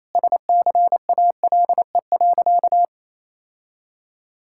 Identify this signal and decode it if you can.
Morse code, heard as a keyed tone, about 28 wpm, 700 Hz. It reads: SCALE.
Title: Sentences from Top 1000 Words - Review Part 5 - 28wpm